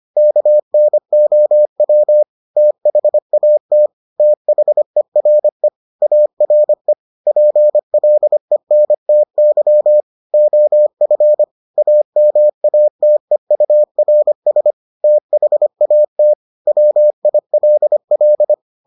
Morse code, heard as a keyed tone, about 25 wpm, 600 Hz. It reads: KNOW THAT THERE ARE PLENTY OF AMATEURS THAT WILL